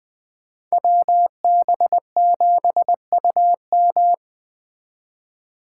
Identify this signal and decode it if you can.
Morse code, heard as a keyed tone, about 20 wpm, 700 Hz. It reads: WB7UM